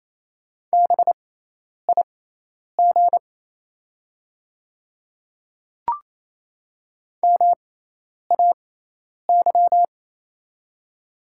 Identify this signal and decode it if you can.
Morse code, heard as a keyed tone, about 28 wpm, spaced as for 10 wpm, 700 Hz. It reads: BIG MAY
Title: 6x Character Spacing - Top 200 Words - 28wpm